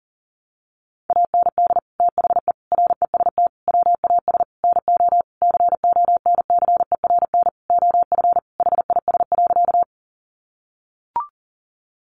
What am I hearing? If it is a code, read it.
Morse code, heard as a keyed tone, about 40 wpm, 700 Hz. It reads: AND THE REST WAS NO CONCERN OF HIS.
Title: Sentences from Top 1000 Words 40wpm